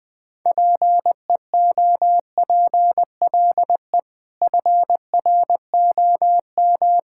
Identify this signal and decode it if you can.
Morse code, heard as a keyed tone, about 21 wpm, 700 Hz. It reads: PEOPLE FROM
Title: Sets of 6 Words - 20wpm